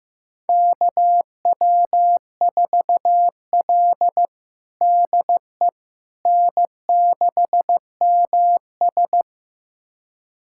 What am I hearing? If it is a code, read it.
Morse code, heard as a keyed tone, about 15 wpm, 700 Hz. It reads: KW4L DE N6MS
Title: USA Callsign DE Callsign - 4 Characters - 15wpm